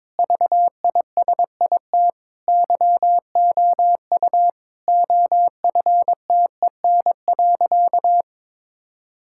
Morse code, heard as a keyed tone, about 22 wpm, 700 Hz. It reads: VISIT YOU OFTEN.
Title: Sentences from Top 1000 Words - Review Part 6 - 22wpm